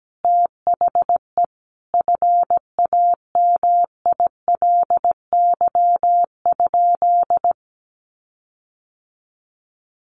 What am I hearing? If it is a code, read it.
Morse code, heard as a keyed tone, about 17 wpm, 700 Hz. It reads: THE FAMILY?
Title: Sentences from Top 1400 Words 17wpm